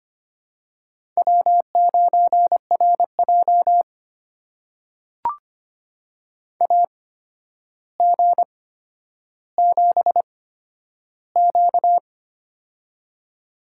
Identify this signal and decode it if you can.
Morse code, heard as a keyed tone, about 25 wpm, spaced as for 11 wpm, 700 Hz. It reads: W9RJ AG7Q
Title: USA Call Signs - 4 Characters - 8x Character Spacing - 25wpm